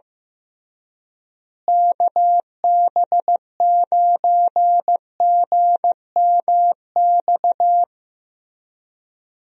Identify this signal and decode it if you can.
Morse code, heard as a keyed tone, about 15 wpm, 700 Hz. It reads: KB9GMX
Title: USA Call Signs - 6 Characters - Repeated 3x - 15wpm